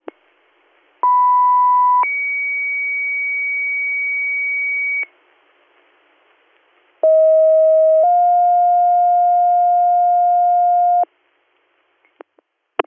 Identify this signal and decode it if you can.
Two-tone page, heard as a one-second tone, then a three-second tone, about 990 then 2250 Hz, about 640 then 710 Hz. It is heard from 1.0 to 11.0 s.